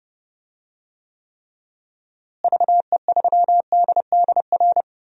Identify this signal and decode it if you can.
Morse code, heard as a keyed tone, about 30 wpm, 700 Hz. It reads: VE3DDR
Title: DXCC-100 Call Signs - 6 Characters - Repeated 3x - 30wpm